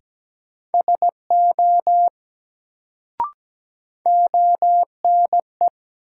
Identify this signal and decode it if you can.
Morse code, heard as a keyed tone, about 17 wpm, 700 Hz. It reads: SO ONE
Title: Sets of 2 Words - New Word Drill - 17wpm